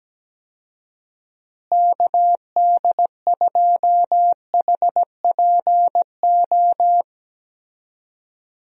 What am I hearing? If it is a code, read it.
Morse code, heard as a keyed tone, about 17 wpm, 700 Hz. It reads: KD2HPO
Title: USA Call Signs - 6 Characters - Repeated 2x - 17wpm